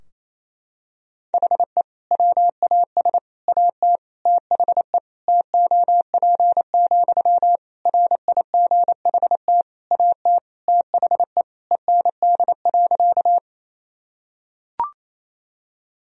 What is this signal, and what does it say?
Morse code, heard as a keyed tone, about 27 wpm, 700 Hz. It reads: HE WAS AT THE TOP, RIGHT AT THE END.
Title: Sentences from Top 1000 Words - Review Part 2 - 28wpm